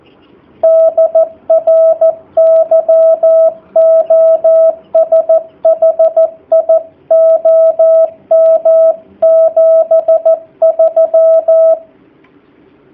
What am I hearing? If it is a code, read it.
Morse code, heard as a keyed tone, about 14 wpm, 660 Hz, with radio noise. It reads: DR YOSHI OM 73
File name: jh8cbh / 2019cw / sp13- w